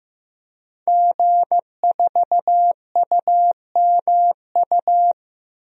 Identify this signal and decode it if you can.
Morse code, heard as a keyed tone, about 15 wpm, 700 Hz. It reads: G4UMU
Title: DXCC-100 Call Signs - 5 Characters - 15wpm